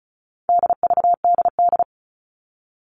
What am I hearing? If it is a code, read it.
Morse code, heard as a keyed tone, about 34 wpm, 700 Hz. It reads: DVDD
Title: All Cut Numbers - 4-Digit 35wpm